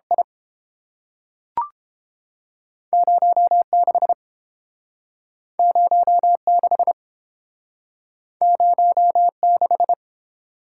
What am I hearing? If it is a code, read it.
Morse code, heard as a keyed tone, about 30 wpm, 700 Hz. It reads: I 0 6 0 6 0 6